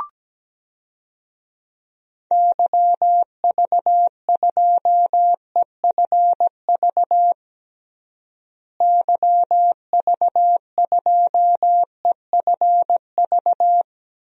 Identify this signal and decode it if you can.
Morse code, heard as a keyed tone, about 17 wpm, 700 Hz. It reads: YV2EFV YV2EFV
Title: DXCC-100 Call Signs - 6 Characters - Repeated 2x - 17wpm